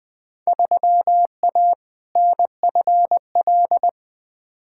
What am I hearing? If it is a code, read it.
Morse code, heard as a keyed tone, about 20 wpm, 700 Hz. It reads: 3A NFL
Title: Field Day Exchanges - 20wpm